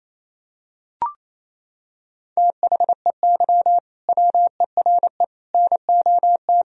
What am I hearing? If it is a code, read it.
Morse code, heard as a keyed tone, about 27 wpm, 700 Hz. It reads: THEY WERE NOT
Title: Sentences from Top 1000 Words - Review Part 3 - 28wpm